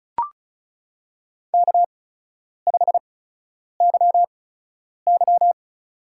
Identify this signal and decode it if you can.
Morse code, heard as a keyed tone, about 34 wpm, spaced as for 10 wpm, 700 Hz. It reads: K5YY
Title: USA Call Signs - 4 Characters - 8x Character Spacing - 35wpm